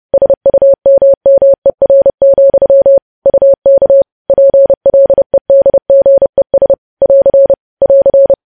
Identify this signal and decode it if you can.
Morse code, heard as a keyed tone, about 30 wpm, 550 Hz. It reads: SUMMER, UK PLEDGES <AR> <AR>
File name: podcast / News headlines-1x-30x30